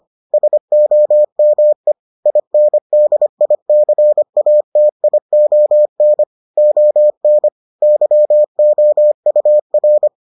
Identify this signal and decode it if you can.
Morse code, heard as a keyed tone, about 25 wpm, 600 Hz. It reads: SOME INDICATION ON YOUR